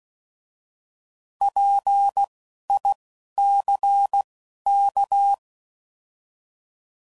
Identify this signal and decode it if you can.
Morse code, heard as a keyed tone, about 16 wpm, spaced as for 12 wpm, 800 Hz. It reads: PICK
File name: Prefixes / By First Letter / Ps